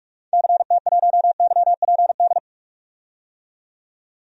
Morse code, heard as a keyed tone, about 46 wpm, 700 Hz. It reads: CT1YPD
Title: DXCC-100 Call Signs - 6 Characters - 45wpm